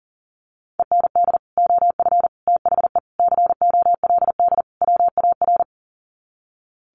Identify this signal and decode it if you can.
Morse code, heard as a keyed tone, about 40 wpm, 700 Hz. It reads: END OF THE COLD WAR
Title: Sets of 5 Words Encore 40wpm